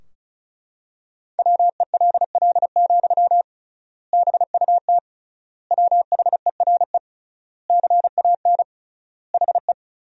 Morse code, heard as a keyed tone, about 36 wpm, 700 Hz. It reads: WELL, BUT WHERE CAN HE
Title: 3x Word-Spacing - Sentences from Top 500 Words - 35wpm